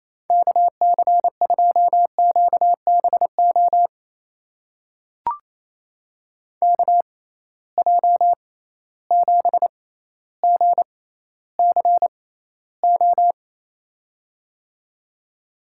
Morse code, heard as a keyed tone, about 28 wpm, spaced as for 14 wpm, 700 Hz. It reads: KC2QBO KJ7GCO
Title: USA Call Signs - 6 Characters - 6x Character Spacing - 28wpm